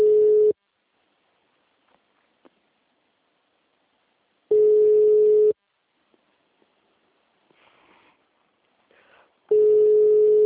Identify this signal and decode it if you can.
Ringback tone (Europe, 425 Hz).